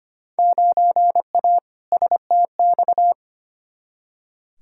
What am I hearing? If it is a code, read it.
Morse code, heard as a keyed tone, about 25 wpm, 700 Hz. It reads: 9A STX
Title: Field Day Exchanges - 25wpm